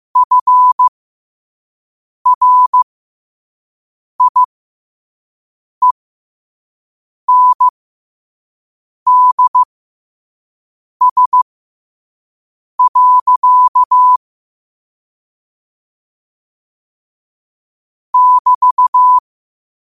Morse code, heard as a keyed tone, about 15 wpm, spaced as for 5 wpm, 1000 Hz. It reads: FRIENDS. <BT>
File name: qotdfiles / 5wpm